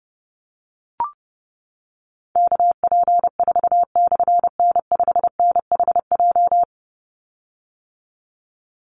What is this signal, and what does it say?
Morse code, heard as a keyed tone, about 30 wpm, 700 Hz. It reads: KP4/N5NHJ
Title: Call Signs - 30wpm